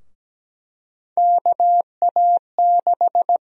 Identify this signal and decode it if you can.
Morse code, heard as a keyed tone, about 17 wpm, 700 Hz. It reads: KA6
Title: USA Call Sign Prefixes - 17wpm